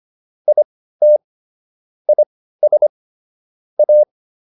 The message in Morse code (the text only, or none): IT IS A